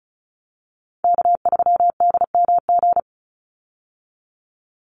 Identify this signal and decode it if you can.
Morse code, heard as a keyed tone, about 34 wpm, 700 Hz. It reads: K3DMG
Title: Call Signs Repeated 2x - 35wpm